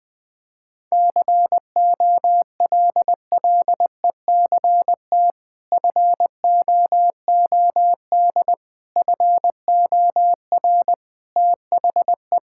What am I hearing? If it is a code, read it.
Morse code, heard as a keyed tone, about 21 wpm, 700 Hz. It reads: COLLECT FOOD FOR THE
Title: Sentences from Top 1600 Words 20wpm